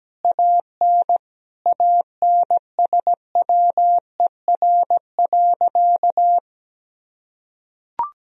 Morse code, heard as a keyed tone, about 17 wpm, 700 Hz. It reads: AN ANSWER.
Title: Sentences from Top 1000 Words - Review Part 4 - 17wpm